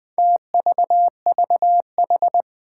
Morse code, heard as a keyed tone, about 20 wpm, 700 Hz. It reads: TVVH
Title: Four Challenging Letter-Number Combinations 20wpm